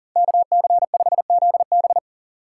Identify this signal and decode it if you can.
Morse code, heard as a keyed tone, about 40 wpm, 700 Hz. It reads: KC5ZB
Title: USA Call Signs - 5 Characters - 2x Character Spacing - 40wpm